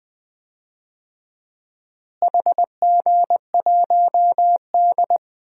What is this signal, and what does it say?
Morse code, heard as a keyed tone, about 20 wpm, 700 Hz. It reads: HG1D
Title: DXCC-100 Call Signs - 4 Characters - Repeated 2x - 20wpm